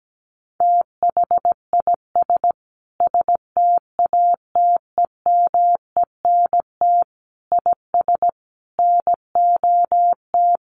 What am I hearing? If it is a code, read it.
Morse code, heard as a keyed tone, about 17 wpm, 700 Hz. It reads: THIS STATEMENT IS NOT
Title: Sentences from Top 1200 Words 17wpm